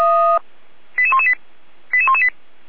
Five-tone selective call (ZVEI), heard as quick runs of five tones, tones of about 70 ms. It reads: 88118 88118
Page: SDR/ZVEI